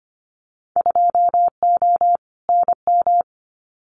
Morse code, heard as a keyed tone, about 25 wpm, 700 Hz. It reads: 2O NM
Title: Winter Field Day Exchanges 25wpm